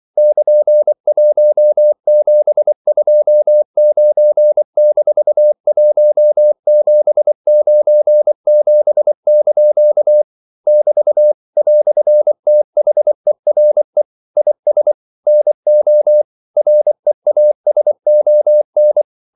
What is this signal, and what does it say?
Morse code, heard as a keyed tone, about 24 wpm, 600 Hz. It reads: <KN>1729-1797) <BT> "THERE IS NO REASON
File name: CW / Practice / EnglishQuotations0424